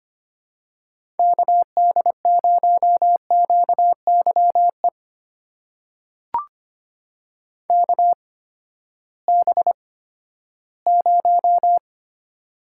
Morse code, heard as a keyed tone, about 25 wpm, spaced as for 14 wpm, 700 Hz. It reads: KD0QYE KB0
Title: USA Call Signs - 6 Characters - 8x Character Spacing - 25wpm